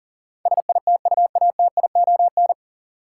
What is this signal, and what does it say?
Morse code, heard as a keyed tone, about 40 wpm, 700 Hz. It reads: SITUATION
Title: Sentences from Top 1200 Words - New Word Drill - 40wpm